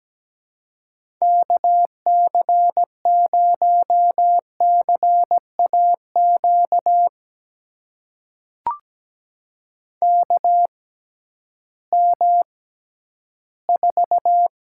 Morse code, heard as a keyed tone, about 17 wpm, spaced as for 11 wpm, 700 Hz. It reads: KC0CAQ KM4
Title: USA Call Signs - 6 Characters - 6x Character Spacing - 17wpm